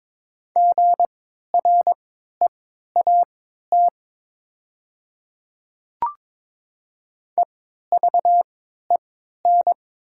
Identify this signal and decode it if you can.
Morse code, heard as a keyed tone, about 22 wpm, spaced as for 13 wpm, 690 Hz. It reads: GREAT EVEN
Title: 3x Character Spacing - Top 300 Words - 22wpm